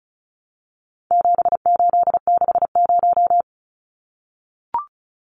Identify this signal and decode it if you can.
Morse code, heard as a keyed tone, about 34 wpm, 700 Hz. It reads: 7860
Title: Numbers - 4-Digit 35wpm